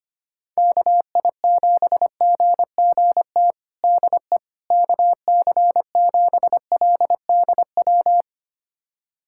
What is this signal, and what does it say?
Morse code, heard as a keyed tone, about 25 wpm, 700 Hz. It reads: KI7GGT DE KC7LDW
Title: USA Callsign DE Callsign - 6 Characters - 25wpm